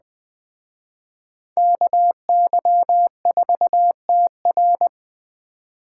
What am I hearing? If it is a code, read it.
Morse code, heard as a keyed tone, about 20 wpm, 700 Hz. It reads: KY4TR